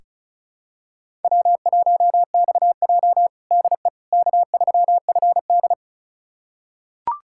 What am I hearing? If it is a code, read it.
Morse code, heard as a keyed tone, about 36 wpm, 700 Hz. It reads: W1XJ DE K3FD
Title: USA Callsign DE Callsign - 4 Characters - 35wpm